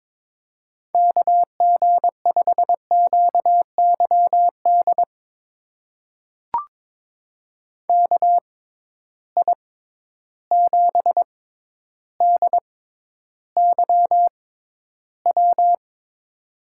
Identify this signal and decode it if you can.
Morse code, heard as a keyed tone, about 23 wpm, spaced as for 11 wpm, 700 Hz. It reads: KG5QYD KI7DYW
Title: USA Call Signs - 6 Characters - 6x Character Spacing - 22wpm